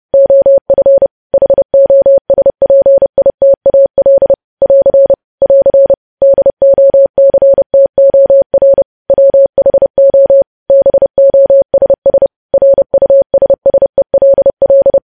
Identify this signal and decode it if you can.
Morse code, heard as a keyed tone, about 30 wpm, 550 Hz. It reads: OF HOSPITAL <AR> <AR> DOCTOR WHO BOSS RUSSELL
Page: news headlines in morse code at 30 wpm